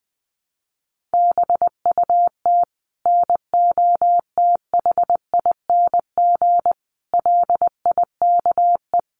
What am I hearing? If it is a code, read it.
Morse code, heard as a keyed tone, about 20 wpm, 700 Hz. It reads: BUT NOTHING LIKE
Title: Sentences from Top 1100 Words 20wpm